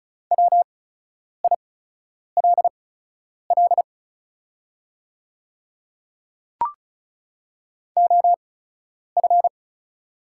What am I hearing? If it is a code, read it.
Morse code, heard as a keyed tone, about 36 wpm, spaced as for 10 wpm, 700 Hz. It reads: WILL OF
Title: 8x Character Spacing - Top 100 Words - 35wpm